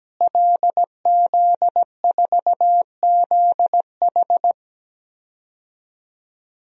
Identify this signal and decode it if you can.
Morse code, heard as a keyed tone, about 17 wpm, 700 Hz. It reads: LZ4ZH
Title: DXCC-100 Call Signs - 5 Characters - Repeated 3x - 17wpm